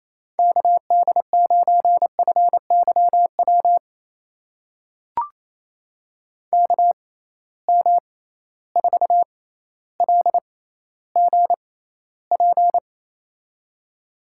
Morse code, heard as a keyed tone, about 28 wpm, spaced as for 14 wpm, 700 Hz. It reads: KD9FYW KM4LGP
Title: USA Call Signs - 6 Characters - 6x Character Spacing - 28wpm